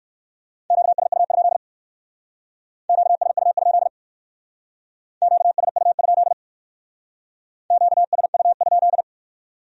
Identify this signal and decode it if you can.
Morse code, heard as a keyed tone, about 58 wpm, 700 Hz. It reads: QSU? QSU? QSU? QSU?